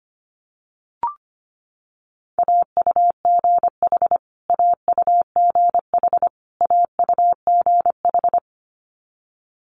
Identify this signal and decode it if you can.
Morse code, heard as a keyed tone, about 25 wpm, 700 Hz. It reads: AUGH AUGH AUGH
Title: Multi-letter Phonemes Repeated 3x 25wpm